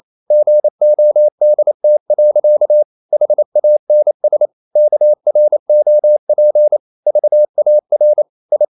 Morse code, heard as a keyed tone, about 28 wpm, 600 Hz. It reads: GODT. HANS KROP VAR I
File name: Ferie QRQ 28wpm